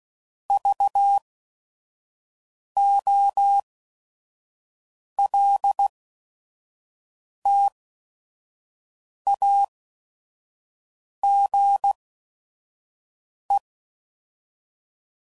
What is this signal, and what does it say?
Morse code, heard as a keyed tone, about 16 wpm, spaced as for 5 wpm, 800 Hz. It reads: VOLTAGE
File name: CW Code Groups @5WPM / Technical Subjects / XC